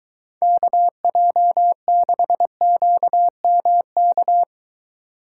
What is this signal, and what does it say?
Morse code, heard as a keyed tone, about 23 wpm, 700 Hz. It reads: KJ6QMK